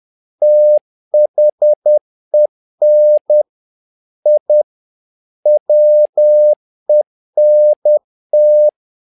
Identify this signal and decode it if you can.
Morse code, heard as a keyed tone, about 10 wpm, 600 Hz. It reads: THEN I WENT
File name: learning center / tower trouble / tower trouble 10wpm